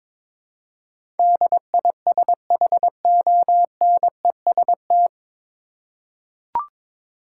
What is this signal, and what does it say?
Morse code, heard as a keyed tone, about 22 wpm, 700 Hz. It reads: DISHONEST
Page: Morse Code Ninja practice 22wpm